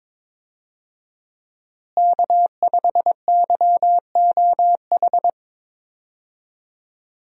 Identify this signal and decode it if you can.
Morse code, heard as a keyed tone, about 22 wpm, 700 Hz. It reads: K5YOH
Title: USA Call Signs - 5 Characters - Repeated 2x - 22wpm